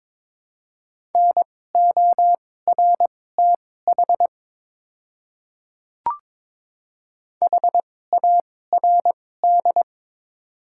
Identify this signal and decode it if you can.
Morse code, heard as a keyed tone, about 22 wpm, spaced as for 16 wpm, 700 Hz. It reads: NORTH HARD
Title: 2x Character Spacing - Top 300 Words - 22wpm